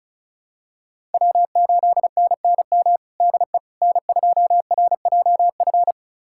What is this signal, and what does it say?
Morse code, heard as a keyed tone, about 34 wpm, 700 Hz. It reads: W8NNM DE N2RJF
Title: USA Callsign DE Callsign - 5 Characters - 35wpm